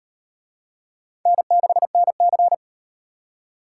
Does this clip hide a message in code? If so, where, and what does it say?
Yes: Morse code from 1.3 to 2.5 s, heard as a keyed tone, about 36 wpm, 700 Hz. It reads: N6NC